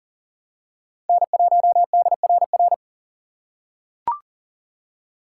Morse code, heard as a keyed tone, about 38 wpm, 700 Hz. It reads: N1DRR
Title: USA Call Signs - 6x Character Spacing - 40wpm